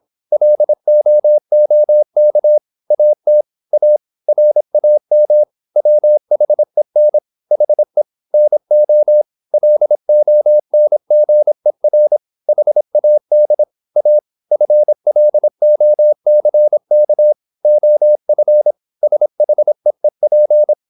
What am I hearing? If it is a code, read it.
Morse code, heard as a keyed tone, about 26 wpm, 600 Hz. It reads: LOOK AT A RAM WHEN HE NO LONGER HAD A FLOCK OF SHEEP
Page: Never 26wpm | CW med Gnister